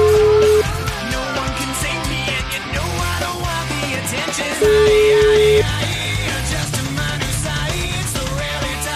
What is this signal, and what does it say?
Ringback tone (Europe, 425 Hz), with noise.